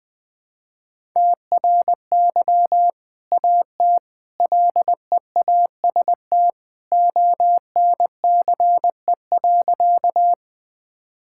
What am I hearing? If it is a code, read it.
Morse code, heard as a keyed tone, about 20 wpm, 700 Hz. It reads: TRY AT LEAST ONCE.
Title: Sentences with 2-Word Phrases - Spoken Beginning - 20wpm